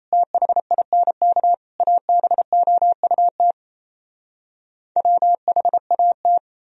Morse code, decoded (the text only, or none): THINK ABOUT WHAT